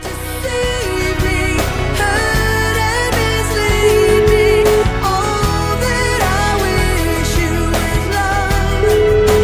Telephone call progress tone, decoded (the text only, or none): ringback tone